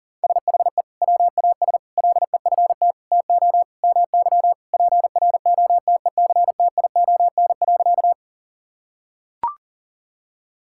Morse code, heard as a keyed tone, about 37 wpm, 700 Hz. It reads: SHE WAS LEFT TO MY PROTECTION.
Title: Sentences from Top 1600 Words 40wpm